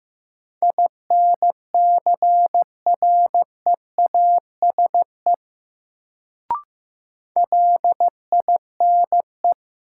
Morse code, heard as a keyed tone, about 15 wpm, 700 Hz. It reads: INCREASE LINE
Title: Sentences from Top 1200 Words - New Word Drill Alt - 15wpm